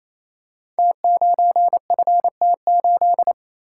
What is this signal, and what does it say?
Morse code, heard as a keyed tone, about 28 wpm, 690 Hz. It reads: T9FT8
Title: Code Groups - 5 Letter:Numbers - 28wpm